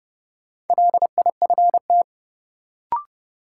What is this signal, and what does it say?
Morse code, heard as a keyed tone, about 31 wpm, 700 Hz. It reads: LIFT
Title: Sentences from Top 1000 Words - New Word Drill - 30wpm